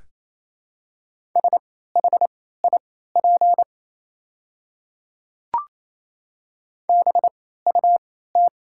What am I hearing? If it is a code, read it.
Morse code, heard as a keyed tone, about 28 wpm, spaced as for 16 wpm, 690 Hz. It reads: SHIP BUT
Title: 3x Character Spacing - Top 500 Words - 28wpm